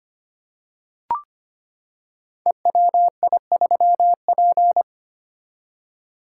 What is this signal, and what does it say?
Morse code, heard as a keyed tone, about 25 wpm, 700 Hz. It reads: EWI3P
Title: Code Groups - 5 Letter:Numbers - 25wpm